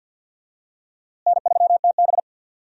Morse code, heard as a keyed tone, about 50 wpm, 700 Hz. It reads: N3TB
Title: USA Call Signs - 4 Characters - Repeated 3x - 50wpm